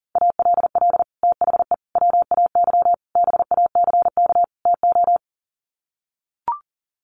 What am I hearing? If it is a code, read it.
Morse code, heard as a keyed tone, about 40 wpm, 700 Hz. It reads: ALL THE WAY BACK TO
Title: Sets of 5 Words Encore 40wpm